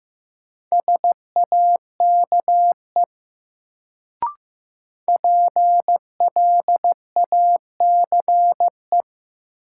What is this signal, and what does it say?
Morse code, heard as a keyed tone, about 15 wpm, 700 Hz. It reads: SAKE PLACE
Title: Sets of 4 Words - New Word Drill - 15wpm